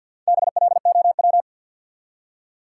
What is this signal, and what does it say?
Morse code, heard as a keyed tone, about 53 wpm, 700 Hz. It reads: BLOW
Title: Four Letter Words - 50wpm